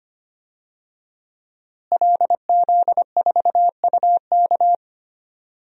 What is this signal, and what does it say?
Morse code, heard as a keyed tone, about 25 wpm, 700 Hz. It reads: LZ4UK
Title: DXCC-100 Call Signs - 5 Characters - Repeated 3x - 25wpm